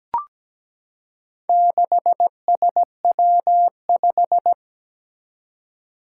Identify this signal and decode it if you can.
Morse code, heard as a keyed tone, about 17 wpm, 700 Hz. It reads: 6SW5
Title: Four Challenging Letter-Number Combinations 17wpm